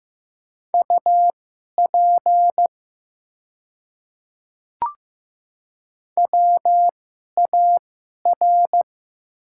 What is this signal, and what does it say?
Morse code, heard as a keyed tone, about 15 wpm, spaced as for 11 wpm, 690 Hz. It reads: UP WAR
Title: 2x Character Spacing - Top 500 Words - 15wpm